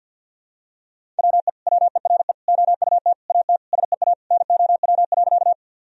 Morse code, heard as a keyed tone, about 50 wpm, 700 Hz. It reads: WE WERE OUT AT SEA NOW.
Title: Sentences from Top 1000 Words - Review Part 1 - 50wpm